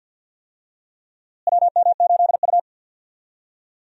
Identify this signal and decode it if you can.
Morse code, heard as a keyed tone, about 50 wpm, 700 Hz. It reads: WM8U
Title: USA Call Signs - 4 Characters - Repeated 3x - 50wpm